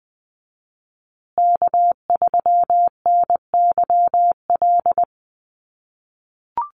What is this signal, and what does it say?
Morse code, heard as a keyed tone, about 19 wpm, 700 Hz. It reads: K3NYL
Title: USA Call Signs Repeated 2x 20wpm